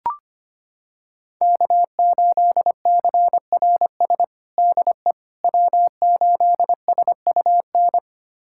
Morse code, heard as a keyed tone, about 25 wpm, 700 Hz. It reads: K8CRS DE W8SUN
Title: USA Callsign DE Callsign - 5 Characters - 25wpm